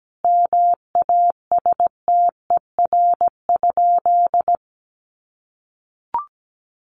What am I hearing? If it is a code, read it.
Morse code, heard as a keyed tone, about 17 wpm, 700 Hz. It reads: MASTER?
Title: Sentences from Top 1300 Words 17wpm